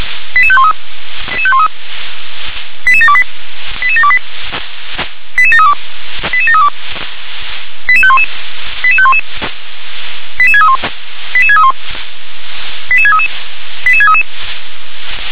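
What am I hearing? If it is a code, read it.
Five-tone selective call (ZVEI), heard as quick runs of five tones, tones of about 70 ms. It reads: 88513 88728 88728 88732 88732 88511 88511 88631 88631 88633 88633
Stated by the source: ZVEI five-tone selective call received with an SDR.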